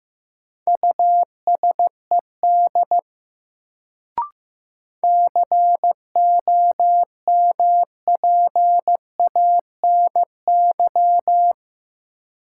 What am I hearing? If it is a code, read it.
Morse code, heard as a keyed tone, about 15 wpm, 700 Hz. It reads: USED COMPANY
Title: Sentences from Top 600 Words - New Word Drill - 15wpm